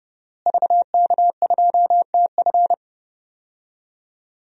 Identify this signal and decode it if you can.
Morse code, heard as a keyed tone, about 30 wpm, 700 Hz. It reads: VK2TF
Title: DXCC-100 Call Signs - 5 Characters - Repeated 3x - 30wpm